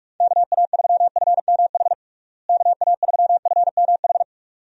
Morse code, heard as a keyed tone, about 46 wpm, 700 Hz. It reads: KA3FMH KA3FMH
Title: USA Call Signs - 6 Characters - Repeated 2x - 45wpm